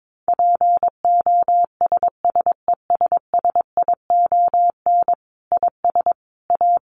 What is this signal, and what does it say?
Morse code, heard as a keyed tone, about 22 wpm, 700 Hz. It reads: POSSESSION IS A
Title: Sentences from Top 1400 Words 22wpm